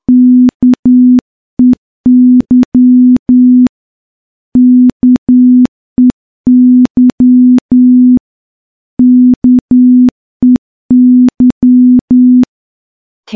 Morse code, heard as a keyed tone, about 10 wpm, 250 Hz. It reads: KEY KEY KEY
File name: morse-training-10WPM